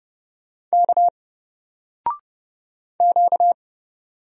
Morse code, heard as a keyed tone, about 30 wpm, 700 Hz. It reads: K Q
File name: Single-Letter-Number-30wpm